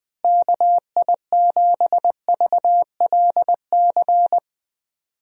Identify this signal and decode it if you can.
Morse code, heard as a keyed tone, about 20 wpm, 700 Hz. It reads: KI7VLC
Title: USA Call Signs - 6 Characters - Repeated 3x - 20wpm